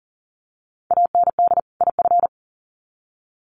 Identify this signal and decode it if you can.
Morse code, heard as a keyed tone, about 40 wpm, 700 Hz. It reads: AND IF
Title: Sets of 2 Words 40wpm